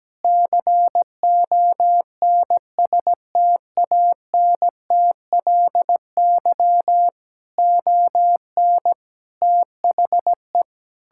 Morse code, heard as a keyed tone, about 17 wpm, 700 Hz. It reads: CONSTANTLY ON THE
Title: Sentences from Top 1600 Words 17wpm